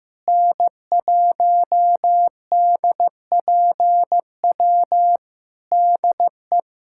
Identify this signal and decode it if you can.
Morse code, heard as a keyed tone, about 15 wpm, 700 Hz. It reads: N1DPW DE
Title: Practice QSO - Exchange 4 - Ending - 15wpm